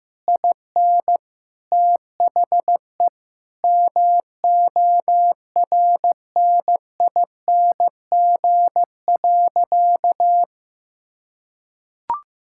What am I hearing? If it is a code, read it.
Morse code, heard as a keyed tone, about 15 wpm, 700 Hz. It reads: IN THE MORNING.
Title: Sentences from Top 1000 Words - Review Part 5 - 15wpm